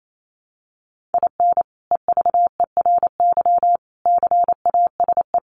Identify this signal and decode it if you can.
Morse code, heard as a keyed tone, about 28 wpm, 700 Hz. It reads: IN EVERY CASE
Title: Sentences from Top 1200 Words 28wpm